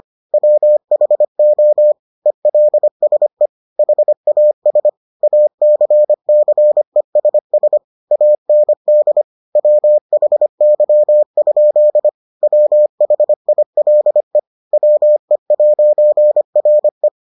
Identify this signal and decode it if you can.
Morse code, heard as a keyed tone, about 25 wpm, 600 Hz. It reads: WHO ELSE HAS ACCESS AND WHY? WHILE WE'RE